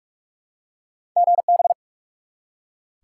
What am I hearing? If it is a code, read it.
Morse code, heard as a keyed tone, about 46 wpm, 700 Hz. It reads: GB
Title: Top 90 QSO Elements 45wpm